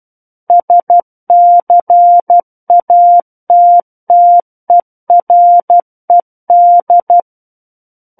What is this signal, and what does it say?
Morse code, heard as a keyed tone, about 12 wpm, 700 Hz. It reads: SCATTERED
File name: Book One / 03-Chapter 3 - 12 WPM